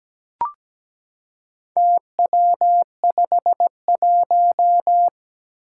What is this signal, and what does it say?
Morse code, heard as a keyed tone, about 17 wpm, 700 Hz. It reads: TW51
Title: Four Challenging Letter-Number Combinations 17wpm